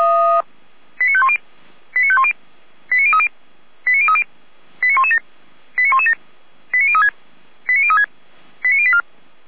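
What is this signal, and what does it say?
Five-tone selective call (ZVEI), heard as quick runs of five tones, tones of about 70 ms. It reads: 89510 89510 89030 89030 89117 89117 89036 89036 89084
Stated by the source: Five-tone ZVEI selective call received with an SDR.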